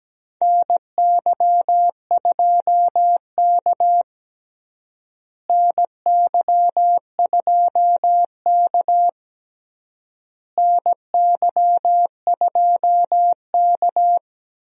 Morse code, heard as a keyed tone, about 17 wpm, 700 Hz. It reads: NY2K NY2K NY2K
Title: USA Call Signs - 4 Characters - Repeated 3x - 17wpm